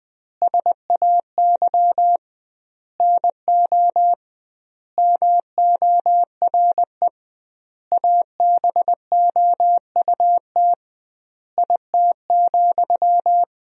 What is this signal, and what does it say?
Morse code, heard as a keyed tone, about 20 wpm, 700 Hz. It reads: SAY NO MORE ABOUT IT,
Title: 2x Word-Spacing - Sentences from Top 200 Words - 20wpm